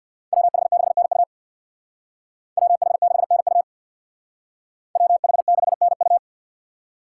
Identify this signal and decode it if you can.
Morse code, heard as a keyed tone, about 55 wpm, 700 Hz. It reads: WH6NU WH6NU WH6NU
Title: Speed Racing - USA 2x2 Call Signs 50wpm